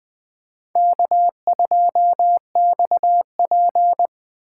Morse code, heard as a keyed tone, about 21 wpm, 700 Hz. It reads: K2XP